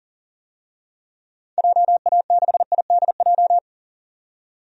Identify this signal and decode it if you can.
Morse code, heard as a keyed tone, about 40 wpm, 700 Hz. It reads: JA6IDJ
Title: DXCC-100 Call Signs - 6 Characters - 40wpm